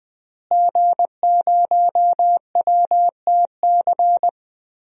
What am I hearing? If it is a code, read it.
Morse code, heard as a keyed tone, about 20 wpm, 700 Hz. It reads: G0WTC
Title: DXCC-100 Call Signs - 5 Characters - Repeated 2x - 20wpm